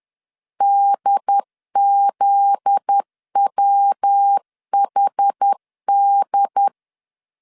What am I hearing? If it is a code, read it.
Morse code, heard as a keyed tone, about 10 wpm, 800 Hz. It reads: DZWHD
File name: LEZIONI / 8 YJ45 / 8 YJ4 10wpm 2633